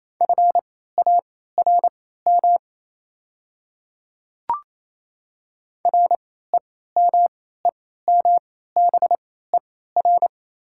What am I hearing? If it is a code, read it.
Morse code, heard as a keyed tone, about 28 wpm, spaced as for 16 wpm, 700 Hz. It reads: FARM REMEMBER
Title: 3x Character Spacing - Top 500 Words - 28wpm